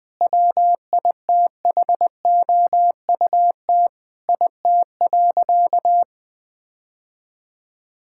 Morse code, heard as a keyed tone, about 20 wpm, 700 Hz. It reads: WITHOUT IT.
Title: Sentences with 5-Word Phrases - 20wpm